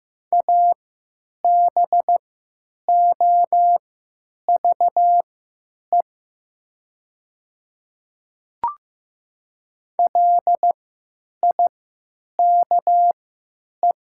Morse code, heard as a keyed tone, about 15 wpm, spaced as for 9 wpm, 700 Hz. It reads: ABOVE LIKE
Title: 3x Character Spacing - Top 500 Words - 15wpm